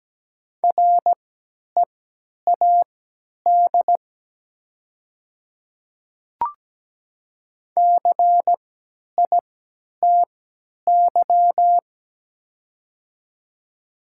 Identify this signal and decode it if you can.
Morse code, heard as a keyed tone, about 17 wpm, spaced as for 10 wpm, 690 Hz. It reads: READ CITY
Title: 3x Character Spacing - Top 300 Words - 17wpm